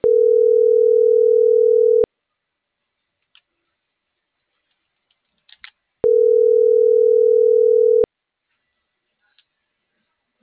Ringback tone (North America), 440 + 480 Hz.